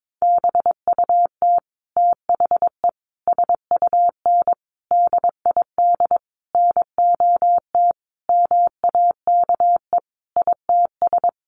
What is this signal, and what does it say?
Morse code, heard as a keyed tone, about 22 wpm, 700 Hz. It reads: BUT THE SUN DID NOT MAKE ITS
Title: Sentences from Top 1100 Words 22wpm